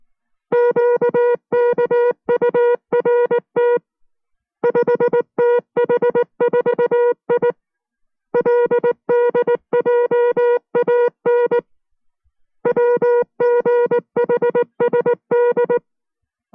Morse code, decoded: QKURT 5TH4I LDJAN WGHSD